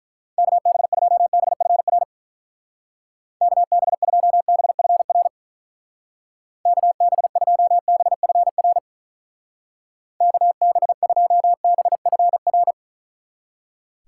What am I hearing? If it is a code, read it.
Morse code, heard as a keyed tone, about 43 wpm, 700 Hz. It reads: KB2BFR KB2BFR KB2BFR KB2BFR
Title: Speed Racing - USA 2x3 Call Signs 35wpm